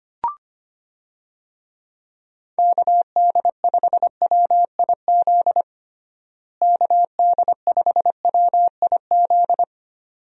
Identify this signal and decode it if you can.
Morse code, heard as a keyed tone, about 25 wpm, 700 Hz. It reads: KD5WIZ KD5WIZ
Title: USA Call Signs - 6 Characters - Repeated 2x - 25wpm